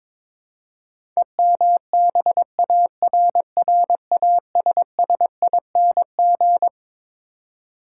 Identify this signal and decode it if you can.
Morse code, heard as a keyed tone, about 21 wpm, 700 Hz. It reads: EMBARRASSING
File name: practice / 22wpm / Twelve Letter Words - 22wpm